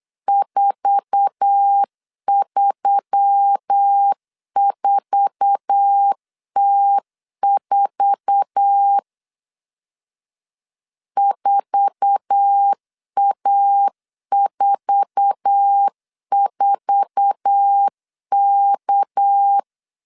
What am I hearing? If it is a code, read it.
Morse code, heard as a keyed tone, about 8 wpm, 800 Hz. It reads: 434T4 4A44K
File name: LEZIONI / 8 YJ45 / 8 YJ4 8wpm 3246